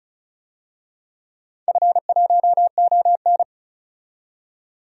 Morse code, heard as a keyed tone, about 34 wpm, 700 Hz. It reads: F1ON
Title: DXCC-100 Call Signs - 4 Characters - 35wpm